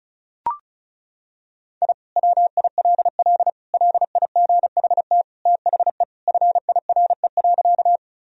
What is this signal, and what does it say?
Morse code, heard as a keyed tone, about 36 wpm, 700 Hz. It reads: I WILL LIGHT THE FIRE.
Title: Sentences from Top 1000 Words - Review Part 1 - 35wpm